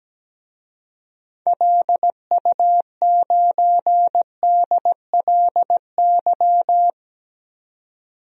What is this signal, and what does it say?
Morse code, heard as a keyed tone, about 17 wpm, 700 Hz. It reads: LU9DLY
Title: DXCC-100 Call Signs - 6 Characters - 17wpm